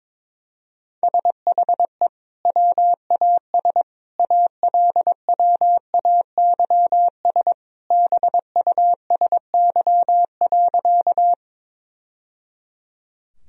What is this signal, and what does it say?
Morse code, heard as a keyed tone, about 22 wpm, 700 Hz. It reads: SHE WAS ALWAYS BUSY.
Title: Sentences from Top 1000 Words - Review Part 3 - 22wpm